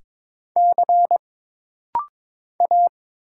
Morse code, heard as a keyed tone, about 22 wpm, 700 Hz. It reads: C A
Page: This rapid-fire timing is geared to push your ICR (Instant Character Recognition) skills.